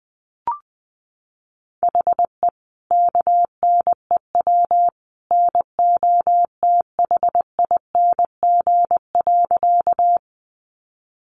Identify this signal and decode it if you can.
Morse code, heard as a keyed tone, about 20 wpm, 700 Hz. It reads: HE KNEW NOTHING.
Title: Sentences from Top 500 Words 20wpm